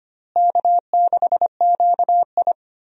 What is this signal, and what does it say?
Morse code, heard as a keyed tone, about 25 wpm, 700 Hz. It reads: K6QI
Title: USA Call Signs - 4 Characters - Repeated 3x - 25wpm